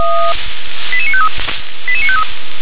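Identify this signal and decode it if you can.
Five-tone selective call (ZVEI), heard as quick runs of five tones, tones of about 70 ms. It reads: 88953 88953
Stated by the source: ZVEI five-tone call 88952 received by SDR.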